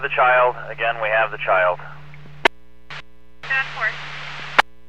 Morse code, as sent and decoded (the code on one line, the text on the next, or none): none